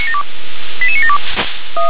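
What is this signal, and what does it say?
Five-tone selective call (ZVEI), heard as quick runs of five tones, tones of about 70 ms. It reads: 88972
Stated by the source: ZVEI five-tone selective call received with an SDR.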